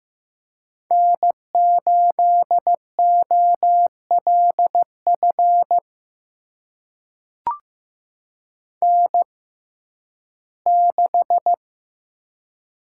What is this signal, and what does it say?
Morse code, heard as a keyed tone, about 15 wpm, spaced as for 11 wpm, 700 Hz. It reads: N8OLF N6
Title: USA Call Signs - 5 Characters - 6x Character Spacing - 15wpm